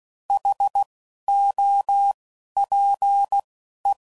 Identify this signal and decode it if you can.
Morse code, heard as a keyed tone, about 16 wpm, spaced as for 11 wpm, 800 Hz. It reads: HOPE